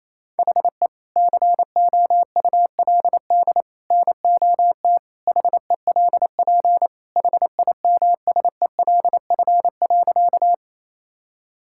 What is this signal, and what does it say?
Morse code, heard as a keyed tone, about 28 wpm, 700 Hz. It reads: HE COULD NOT HELP HIMSELF.
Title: Sentences from Top 1600 Words 28wpm